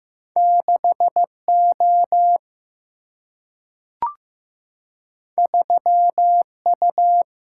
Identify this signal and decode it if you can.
Morse code, heard as a keyed tone, about 15 wpm, 700 Hz. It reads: 6O 3U
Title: Single-Letter Single-Number 15wpm